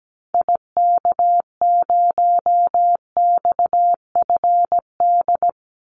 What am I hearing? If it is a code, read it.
Morse code, heard as a keyed tone, about 17 wpm, 700 Hz. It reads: IK0XFD
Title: Call Signs - 17wpm